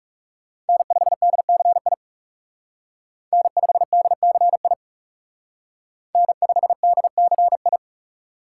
Morse code, heard as a keyed tone, about 39 wpm, 700 Hz. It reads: N5DCI N5DCI N5DCI